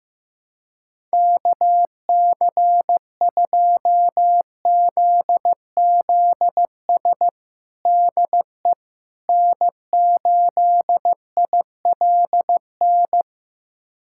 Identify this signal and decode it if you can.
Morse code, heard as a keyed tone, about 15 wpm, 700 Hz. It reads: KC2ZZS DE N8ILN
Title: USA Callsign DE Callsign 15wpm